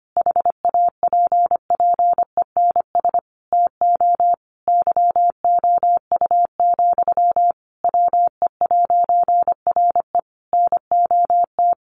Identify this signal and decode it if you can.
Morse code, heard as a keyed tone, about 25 wpm, 700 Hz. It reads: HAPPENS TO YOU, WE'RE NOT
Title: A Sound Of Thunder 25wpm